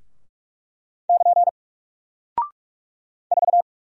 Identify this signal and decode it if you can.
Morse code, heard as a keyed tone, about 46 wpm, 700 Hz. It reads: <KN> 4
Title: Single Character (Letter:Number:Punctuation:Prosign) Alt 45wpm